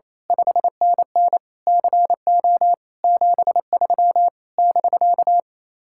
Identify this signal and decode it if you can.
Morse code, heard as a keyed tone, about 27 wpm, 700 Hz. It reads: 5NN CO 73 <BK>
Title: Speed-Racing Common QSO Phrases Encore - 28wpm